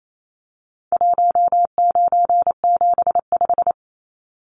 Morse code, heard as a keyed tone, about 28 wpm, 700 Hz. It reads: 1975
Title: Numbers - 4-Digit 28wpm